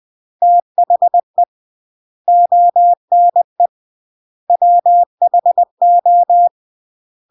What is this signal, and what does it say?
Morse code, heard as a keyed tone, about 20 wpm, 700 Hz. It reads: THE ONE WHO
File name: CW / jodorowsky / 82 commandments 20-20